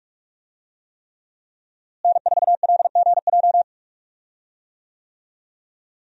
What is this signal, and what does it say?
Morse code, heard as a keyed tone, about 46 wpm, 700 Hz. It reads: N4LGJ
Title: USA Call Signs - 5 Characters - Repeated 3x - 45wpm